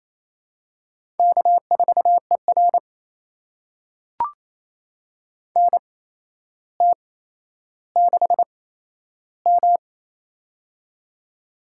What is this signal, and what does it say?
Morse code, heard as a keyed tone, about 28 wpm, spaced as for 12 wpm, 700 Hz. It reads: K4ER NT6M
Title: USA Call Signs - 4 Characters - 8x Character Spacing - 28wpm